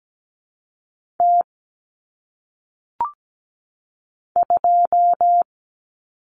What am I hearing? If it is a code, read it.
Morse code, heard as a keyed tone, about 17 wpm, 700 Hz. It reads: T 2
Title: Common Cut Numbers - 1-Digit 17wpm